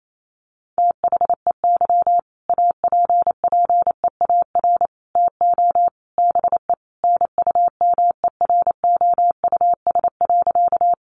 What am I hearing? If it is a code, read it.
Morse code, heard as a keyed tone, about 28 wpm, 700 Hz. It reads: THEY APPEAR TO BE NUMEROUS.
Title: Sentences from Top 1400 Words 28wpm